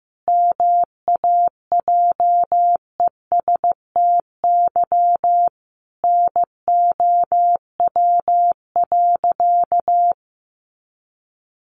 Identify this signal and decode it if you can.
Morse code, heard as a keyed tone, about 15 wpm, 700 Hz. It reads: MAJESTY NOW.
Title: A Sound Of Thunder 15wpm